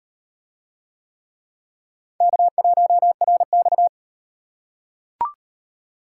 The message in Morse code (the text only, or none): K1RX